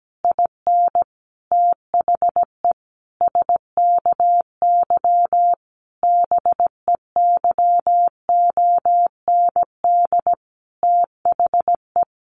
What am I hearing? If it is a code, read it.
Morse code, heard as a keyed tone, about 17 wpm, 700 Hz. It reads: IN THE SKY BEYOND THE
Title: A Sound Of Thunder 17wpm